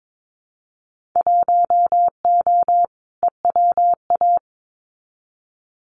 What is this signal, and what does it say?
Morse code, heard as a keyed tone, about 22 wpm, 700 Hz. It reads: 1O EWA
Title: Winter Field Day Exchanges Encore 22wpm